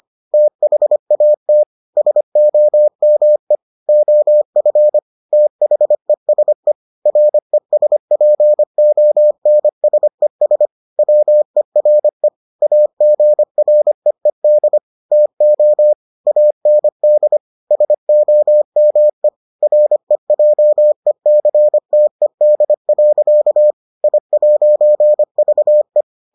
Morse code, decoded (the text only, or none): THAT SOME OF THESE RESPONSES WERE AGREED TO AND SOME REJECTED. I'VE